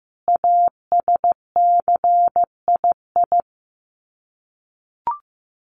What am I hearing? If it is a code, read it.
Morse code, heard as a keyed tone, about 15 wpm, 700 Hz. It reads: ASCII
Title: Sets of 5 Words 15wpm